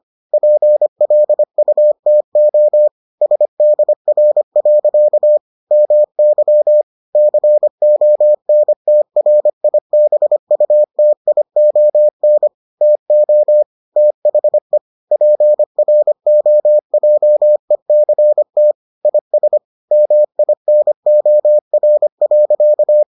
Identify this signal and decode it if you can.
Morse code, heard as a keyed tone, about 25 wpm, 600 Hz. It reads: PLUTO SDR. MY CONTRIBUTION TO THE PROJECT IS MINOR.